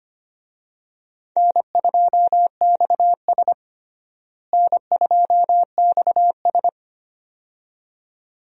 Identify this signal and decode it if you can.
Morse code, heard as a keyed tone, about 25 wpm, 700 Hz. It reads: N2XS N2XS
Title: USA Call Signs - 4 Characters - Repeated 2x - 25wpm